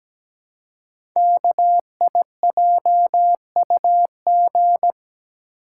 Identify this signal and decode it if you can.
Morse code, heard as a keyed tone, about 17 wpm, 700 Hz. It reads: KIJUG
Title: Code Groups - 5 Letters - 17wpm